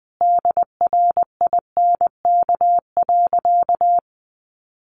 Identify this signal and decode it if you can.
Morse code, heard as a keyed tone, about 20 wpm, 700 Hz. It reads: DRINK.
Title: Sentences from Top 1500 Words 20wpm